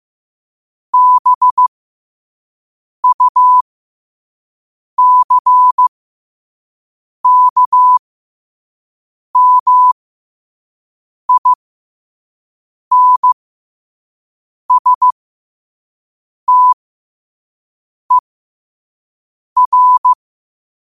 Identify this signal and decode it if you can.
Morse code, heard as a keyed tone, about 15 wpm, spaced as for 5 wpm, 1000 Hz. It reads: BUCKMINSTER
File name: qotdfiles / 5wpm